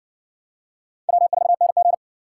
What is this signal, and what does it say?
Morse code, heard as a keyed tone, about 58 wpm, 700 Hz. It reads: W4NG